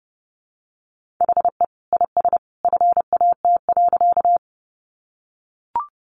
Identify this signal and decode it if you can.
Morse code, heard as a keyed tone, about 30 wpm, 700 Hz. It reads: HE IS FAT.
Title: Sentences from Top 600 Words 30wpm